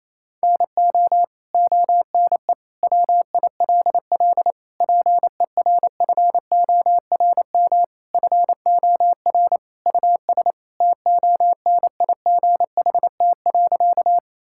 Morse code, heard as a keyed tone, about 28 wpm, 700 Hz. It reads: NO ONE WILL PERFORM FOR US TONIGHT.
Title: Sentences from Top 1000 Words - Review Part 5 - 28wpm